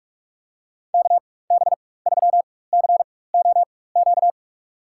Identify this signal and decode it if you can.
Morse code, heard as a keyed tone, about 42 wpm, spaced as for 20 wpm, 690 Hz. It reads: KB3COQ
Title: USA Call Signs - 4x Character Spacing - 45wpm